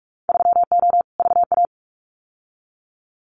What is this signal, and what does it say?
Morse code, heard as a keyed tone, about 44 wpm, 700 Hz. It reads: 3O VA
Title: Winter Field Day Exchanges 45wpm